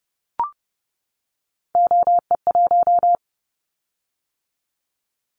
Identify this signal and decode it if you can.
Morse code, heard as a keyed tone, about 30 wpm, 700 Hz. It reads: OE1
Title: Callsign Prefixes for Top 100 DXCC - 30wpm